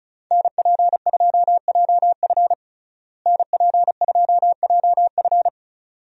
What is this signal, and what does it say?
Morse code, heard as a keyed tone, about 36 wpm, 700 Hz. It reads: NP2JF NP2JF
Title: USA Call Signs - 5 Characters - Repeated 2x - 35wpm